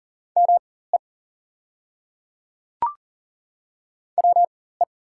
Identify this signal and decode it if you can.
Morse code, heard as a keyed tone, about 37 wpm, spaced as for 18 wpm, 700 Hz. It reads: ME WE
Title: 4x Character Spacing - Top 200 Words - 40wpm